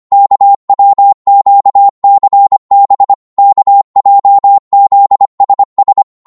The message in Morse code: KWQCB KJZSS